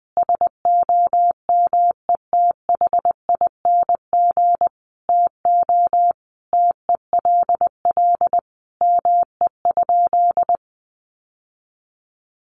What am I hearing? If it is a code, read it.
Morse code, heard as a keyed tone, about 20 wpm, 700 Hz. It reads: SOMETHING TO TELL ME?
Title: Sentences from Top 600 Words 20wpm